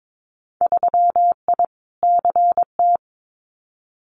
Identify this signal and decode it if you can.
Morse code, heard as a keyed tone, about 22 wpm, 700 Hz. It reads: 3I CT
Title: Winter Field Day Exchanges 22wpm